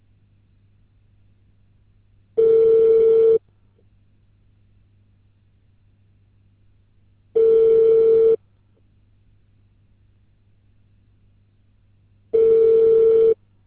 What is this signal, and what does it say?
Ringback tone (China, 450 Hz).